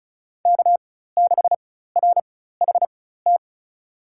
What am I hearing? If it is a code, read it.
Morse code, heard as a keyed tone, about 36 wpm, spaced as for 16 wpm, 700 Hz. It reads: K6RHT
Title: USA Call Signs - 4x Character Spacing - 35wpm